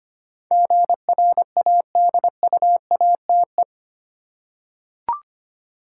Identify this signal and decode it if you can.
Morse code, heard as a keyed tone, about 25 wpm, 700 Hz. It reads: GRADUATE
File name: Eight Letter Words - 25wpm